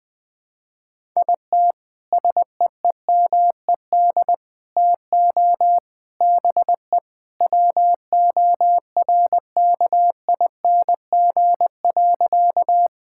Morse code, heard as a keyed tone, about 20 wpm, 700 Hz. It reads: IT SEEMED TO BE WORKING.
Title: Sentences with 3-Word Phrases - 20wpm